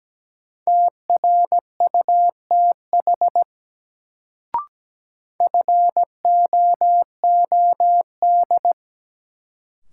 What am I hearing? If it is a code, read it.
Morse code, heard as a keyed tone, about 17 wpm, 700 Hz. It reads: TRUTH FOOD
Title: Sets of 5 Words - New Word Drill - 17wpm